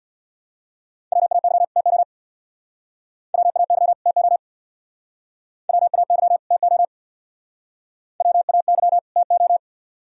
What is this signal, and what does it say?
Morse code, heard as a keyed tone, about 62 wpm, 700 Hz. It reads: WAY TO WAY TO WAY TO WAY TO
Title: Speed-Racing - Sets of 2 Words 50wpm